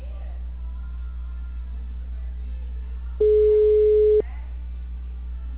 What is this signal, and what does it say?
Ringback tone (Europe, 425 Hz).
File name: out- -2020- - - .29659